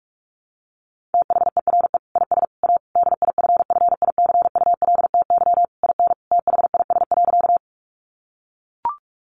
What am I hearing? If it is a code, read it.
Morse code, heard as a keyed tone, about 44 wpm, 700 Hz. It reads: THERE IS A DIFFICULTY IN THIS.
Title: Sentences from Top 1500 Words 45wpm